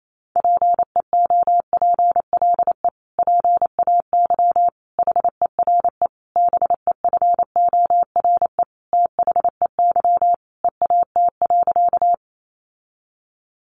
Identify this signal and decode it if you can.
Morse code, heard as a keyed tone, about 28 wpm, 700 Hz. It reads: PEOPLE PAY HERE BEFORE THEY EAT.
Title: Sentences from Top 700 Words 28wpm